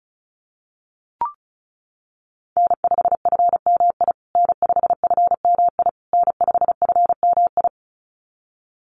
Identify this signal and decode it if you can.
Morse code, heard as a keyed tone, about 34 wpm, 700 Hz. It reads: N5FMI N5FMI N5FMI
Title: USA Call Signs Repeated 3x 35wpm